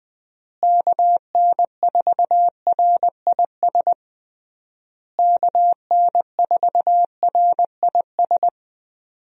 Morse code, heard as a keyed tone, about 21 wpm, 700 Hz. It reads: KN4RIS KN4RIS
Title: USA Call Signs - 6 Characters - Repeated 3x - 20wpm